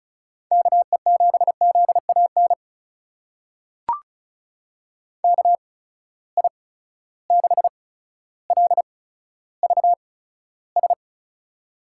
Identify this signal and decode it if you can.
Morse code, heard as a keyed tone, about 36 wpm, spaced as for 15 wpm, 700 Hz. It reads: KE7ZAN KI6LVS
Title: USA Call Signs - 6 Characters - 8x Character Spacing - 35wpm